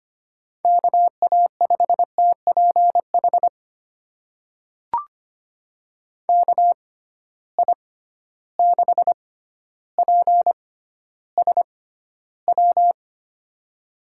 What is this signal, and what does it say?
Morse code, heard as a keyed tone, about 25 wpm, spaced as for 13 wpm, 700 Hz. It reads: KA5TPH KI6PSW
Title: USA Call Signs - 6 Characters - 6x Character Spacing - 25wpm